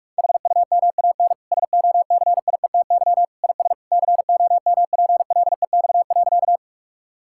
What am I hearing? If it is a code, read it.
Morse code, heard as a keyed tone, about 46 wpm, 700 Hz. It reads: HUMAN SOCIETY IS COMPLEX.
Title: Sentences from Top 1000 Words - Review Part 6 - 45wpm